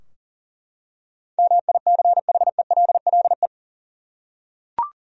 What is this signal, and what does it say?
Morse code, heard as a keyed tone, about 40 wpm, 700 Hz. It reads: MICHELLE
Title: USA Common Names - 40wpm